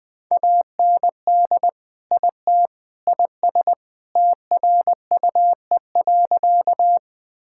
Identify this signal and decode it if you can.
Morse code, heard as a keyed tone, about 20 wpm, 700 Hz. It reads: AND IT IS TRUE.
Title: Sentences with 3-Word Phrases - 20wpm